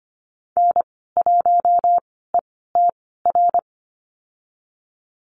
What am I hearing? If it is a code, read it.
Morse code, heard as a keyed tone, about 25 wpm, spaced as for 16 wpm, 700 Hz. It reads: N1ETR
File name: Lesson 15g - Number 5 - Call Signs 25wpm